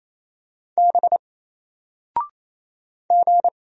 Morse code, heard as a keyed tone, about 27 wpm, 700 Hz. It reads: B G
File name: Single-Letter-Number-28wpm